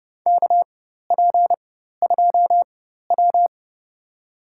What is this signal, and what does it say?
Morse code, heard as a keyed tone, about 30 wpm, spaced as for 14 wpm, 700 Hz. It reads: KP2W
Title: USA Call Signs - 4 Characters - 4x Character Spacing - 30wpm